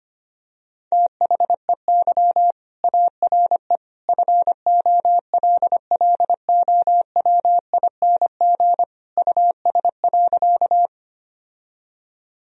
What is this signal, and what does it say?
Morse code, heard as a keyed tone, about 25 wpm, 700 Hz. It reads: THEY ARE FOLLOWING US.
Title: Sentences from Top 1000 Words - Review Part 4 - 25wpm